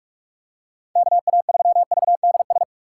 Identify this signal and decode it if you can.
Morse code, heard as a keyed tone, about 46 wpm, 700 Hz. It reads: KA3VDS
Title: USA Call Signs - 4x Character Spacing - 45wpm